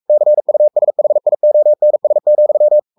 Morse code, heard as a keyed tone, about 44 wpm, 600 Hz. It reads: CUSHIONS,
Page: Romanen Never af Ken Follet på engelsk.